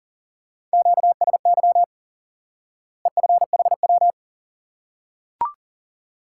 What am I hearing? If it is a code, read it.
Morse code, heard as a keyed tone, about 37 wpm, 700 Hz. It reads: QSY EFHW
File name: IQE Course 3 - Lesson 89 - YAGI - Two Elements - 40wpm